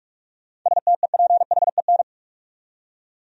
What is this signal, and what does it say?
Morse code, heard as a keyed tone, about 41 wpm, 700 Hz. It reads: STEPHEN
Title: USA Common Names - 45wpm